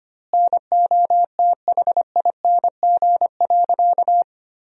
Morse code, heard as a keyed tone, about 25 wpm, 700 Hz. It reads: NOTHING.
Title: 2x Word-Spacing - Sentences from Top 500 Words - 25wpm